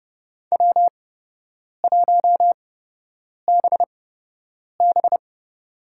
Morse code, heard as a keyed tone, about 30 wpm, spaced as for 8 wpm, 700 Hz. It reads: W1BB
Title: USA Call Signs - 4 Characters - 8x Character Spacing - 30wpm